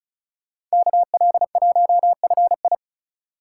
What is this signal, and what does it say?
Morse code, heard as a keyed tone, about 34 wpm, 700 Hz. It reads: KL1FI
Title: Speed Racing - USA 2x2 Call Signs 30wpm